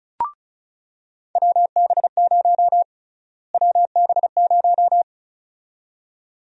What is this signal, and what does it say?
Morse code, heard as a keyed tone, about 34 wpm, 700 Hz. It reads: WB0 WB0
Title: USA Call Sign Prefixes - Repeated 2x - 35wpm